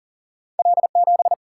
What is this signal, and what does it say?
Morse code, heard as a keyed tone, about 39 wpm, 700 Hz. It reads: L 7
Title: Single-Letter Single-Number 40wpm